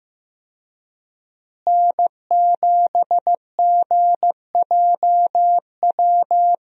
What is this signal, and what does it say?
Morse code, heard as a keyed tone, about 15 wpm, 700 Hz. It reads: N7GJW
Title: USA Call Signs - 5 Characters - Repeated 2x - 15wpm